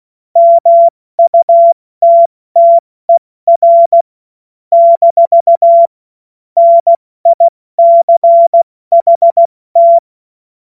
Morse code, heard as a keyed tone, about 16 wpm, 680 Hz. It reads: MUTTER - NICHT
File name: Morsekurs / Witze / Witz 10. August 2021, 70 F80